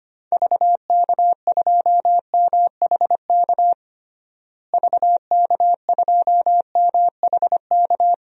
Morse code, heard as a keyed tone, about 25 wpm, 700 Hz. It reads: VK2MHK VK2MHK
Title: DXCC-100 Call Signs - 6 Characters - Repeated 2x - 25wpm